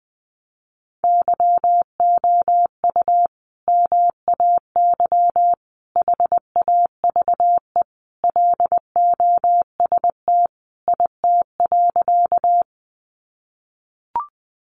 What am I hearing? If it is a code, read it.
Morse code, heard as a keyed tone, about 20 wpm, 700 Hz. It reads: YOU MAY HAVE LOST IT.
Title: Sentences from Top 1400 Words 20wpm